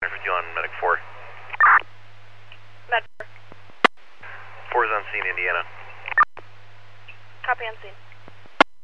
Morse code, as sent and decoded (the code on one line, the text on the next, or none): none